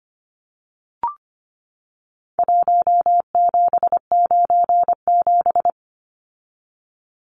Morse code, heard as a keyed tone, about 25 wpm, 700 Hz. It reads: 1797
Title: Numbers - Contest Style - 4-Digit 25wpm